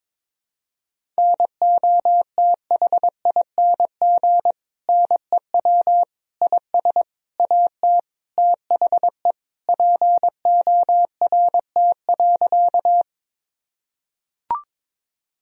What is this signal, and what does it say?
Morse code, heard as a keyed tone, about 22 wpm, 700 Hz. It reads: NOTHING NEW IS AT THE PORT.
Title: Sentences from Top 1000 Words - Review Part 1 - 22wpm